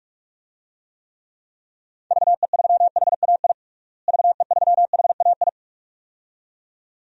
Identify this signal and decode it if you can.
Morse code, heard as a keyed tone, about 46 wpm, 700 Hz. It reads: VE3HAI VE3HAI
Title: DXCC-100 Call Signs - 6 Characters - Repeated 2x - 45wpm